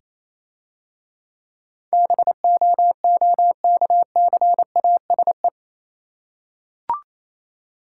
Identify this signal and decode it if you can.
Morse code, heard as a keyed tone, about 27 wpm, 700 Hz. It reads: BOOKCASE
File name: Easy Words - Speed-Racing with Hints - 22wpm